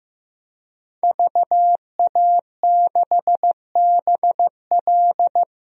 Morse code, heard as a keyed tone, about 15 wpm, 700 Hz. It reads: VA6BL
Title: DXCC-100 Call Signs - 5 Characters - 15wpm